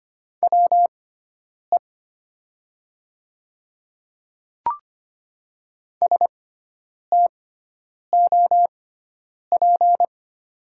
Morse code, heard as a keyed tone, about 25 wpm, spaced as for 9 wpm, 700 Hz. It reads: WE STOP
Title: 6x Character Spacing - Top 300 Words - 25wpm